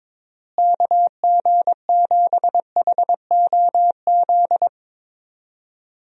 Morse code, heard as a keyed tone, about 21 wpm, 700 Hz. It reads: KG7HOZ
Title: USA Call Signs - 6 Characters - Repeated 3x - 22wpm